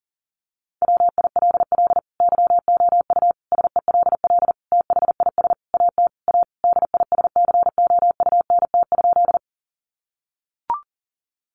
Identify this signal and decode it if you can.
Morse code, heard as a keyed tone, about 40 wpm, 700 Hz. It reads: WILL YOU SELL THIS AT A DISCOUNT?
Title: Sentences from Top 1200 Words 40wpm